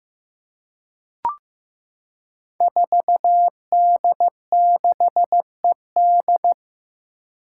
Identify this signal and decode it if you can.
Morse code, heard as a keyed tone, about 15 wpm, 700 Hz. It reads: 4D6ED
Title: Five Challenging Letter-Number Combinations 15wpm